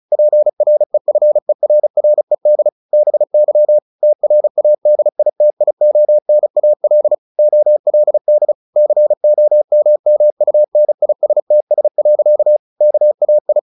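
Morse code, heard as a keyed tone, about 34 wpm, 600 Hz. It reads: PREFERRED BY TRADITIONAL OLD COMMUNISTS. KAI
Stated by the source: Morse practice at 35 wpm from CW med Gnister, sending Ken Follett's novel Never in English.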